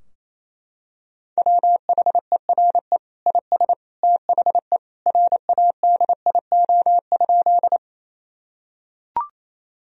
Morse code, heard as a keyed tone, about 28 wpm, 700 Hz. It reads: WHERE IS THE RADIO?
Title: Sentences from Top 1000 Words - Review Part 3 - 28wpm